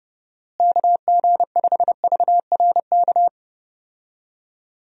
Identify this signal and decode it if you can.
Morse code, heard as a keyed tone, about 30 wpm, 700 Hz. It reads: KG5VRK
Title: USA Call Signs - 6 Characters - Repeated 2x - 30wpm